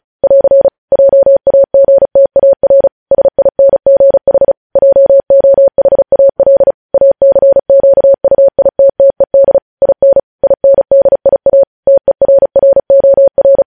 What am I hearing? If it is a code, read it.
Morse code, heard as a keyed tone, about 36 wpm, 550 Hz. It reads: <AR> JAGTAR SINGH JOHAL ACQUITTED IN INDIA TERROR
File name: podcast / News headlines-1x-35x35